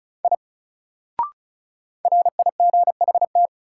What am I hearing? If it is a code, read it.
Morse code, heard as a keyed tone, about 35 wpm, 700 Hz. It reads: I RIGHT
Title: Sentences from Top 200 Words - New Word Drill - 35wpm